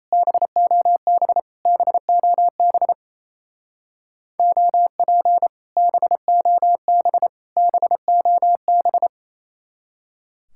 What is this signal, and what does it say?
Morse code, heard as a keyed tone, about 35 wpm, 700 Hz. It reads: BOB BOB OP BOB BOB